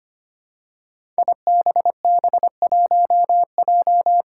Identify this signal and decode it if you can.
Morse code, heard as a keyed tone, about 25 wpm, 700 Hz. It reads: IBB1J
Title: Five Challenging Letter-Number Combinations 25wpm